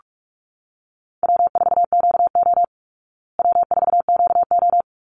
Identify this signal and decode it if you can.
Morse code, heard as a keyed tone, about 46 wpm, 700 Hz. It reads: W4QO W4QO
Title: Call Signs Repeated 3x - 45wpm